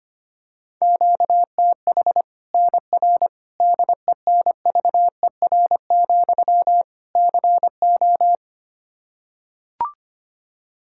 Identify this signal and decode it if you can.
Morse code, heard as a keyed tone, about 25 wpm, 700 Hz. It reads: QTH NR DENVER, CO
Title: K1USN SST QSO Practice 25wpm